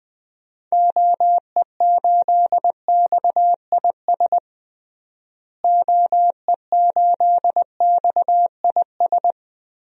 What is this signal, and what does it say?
Morse code, heard as a keyed tone, about 21 wpm, 700 Hz. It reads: OE8XIS OE8XIS
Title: DXCC-100 Call Signs - 6 Characters - Repeated 3x - 20wpm